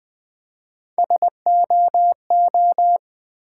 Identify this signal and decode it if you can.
Morse code, heard as a keyed tone, about 20 wpm, 700 Hz. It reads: SOO
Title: Build A Word - 1 Letter At A Time - 4-Letter Words - 20wpm